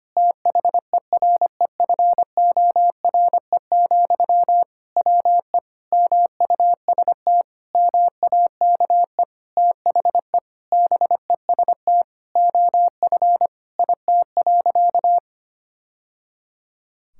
Morse code, heard as a keyed tone, about 25 wpm, 700 Hz. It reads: THEREFORE, WE MUST MAKE THE BEST OF IT.
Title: Sentences from Top 1000 Words - Review Part 5 - 25wpm